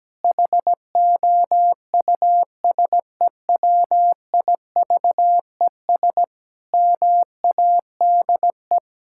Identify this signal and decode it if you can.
Morse code, heard as a keyed tone, about 17 wpm, 700 Hz. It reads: HOUSEWIVES MADE
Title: Pangram Challenge v2 - 17wpm